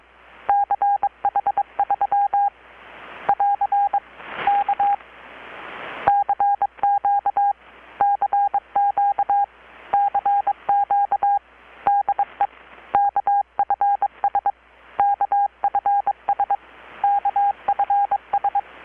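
Morse code, heard as a keyed tone, about 22 wpm, 810 Hz. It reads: CH3 <AR> K CQ CQ CQ DE KFS KFS KFS
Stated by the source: Morse from KPH's Night of Nights XX maritime radiotelegraphy event, received in Chiapas, Mexico about 3,800 km away on a Kenwood R-600 with a grounded 10 m long-wire antenna.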